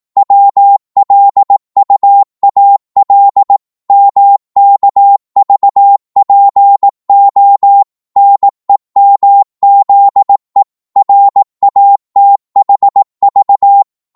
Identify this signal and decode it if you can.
Morse code, heard as a keyed tone, about 18 wpm, 800 Hz. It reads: WLUAL MKVPO NEMZE RATHV
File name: Morse / ltr-18wpm